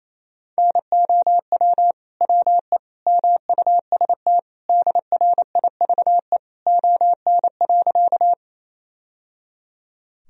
Morse code, heard as a keyed tone, about 28 wpm, 700 Hz. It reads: NOW WE MUST DRIVE ON.
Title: Sentences from Top 1000 Words - Review Part 2 - 28wpm